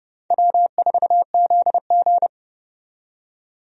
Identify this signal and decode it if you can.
Morse code, heard as a keyed tone, about 30 wpm, 700 Hz. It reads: W4ZG
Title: USA Call Signs - 4 Characters - 30wpm